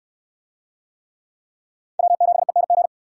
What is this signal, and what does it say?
Morse code, heard as a keyed tone, about 65 wpm, 700 Hz. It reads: W7ENG